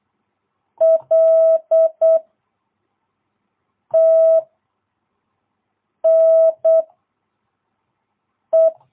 Morse code, heard as a keyed tone, about 8 wpm, 650 Hz. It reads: L T N E
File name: jh8cbh / 2022cw